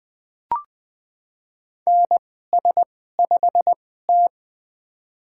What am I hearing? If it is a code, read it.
Morse code, heard as a keyed tone, about 21 wpm, spaced as for 15 wpm, 700 Hz. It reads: NS5T
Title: USA Call Signs - 4 Characters - 2x Character Spacing - 20wpm